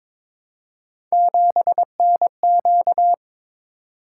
Morse code, heard as a keyed tone, about 22 wpm, 700 Hz. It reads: 7NQ
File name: practice / 22wpm / Three Letter-Number Combinations - 22wpm